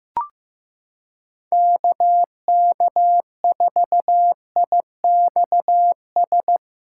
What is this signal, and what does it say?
Morse code, heard as a keyed tone, about 15 wpm, 700 Hz. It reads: KK4IXS
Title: USA Callsign DE Callsign - 6 Characters - 15wpm